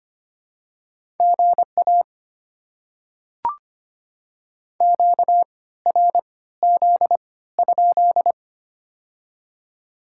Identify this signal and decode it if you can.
Morse code, heard as a keyed tone, about 25 wpm, spaced as for 16 wpm, 690 Hz. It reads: GA QRZ?
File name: Top 90 QSO Elements - 3x Character Spacing - 25wpm